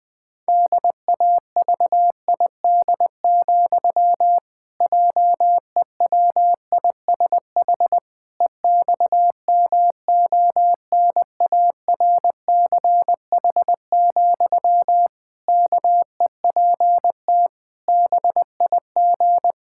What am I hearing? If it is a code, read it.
Morse code, heard as a keyed tone, about 21 wpm, 700 Hz. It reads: DAVID, JEWISH EXMONARCH, KEPT BIG
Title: Pangram Challenge v2 - 20wpm